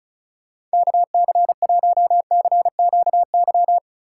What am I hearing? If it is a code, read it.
Morse code, heard as a keyed tone, about 34 wpm, 700 Hz. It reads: KC1CQY
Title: USA Call Signs - 6 Characters - 4x Character Spacing - 35wpm